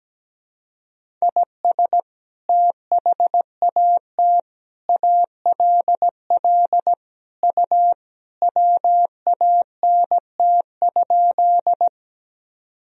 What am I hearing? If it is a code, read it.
Morse code, heard as a keyed tone, about 17 wpm, 700 Hz. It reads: IS THAT ALL U WANT?
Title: Sentences from Top 200 Words and Abbreviations 17wpm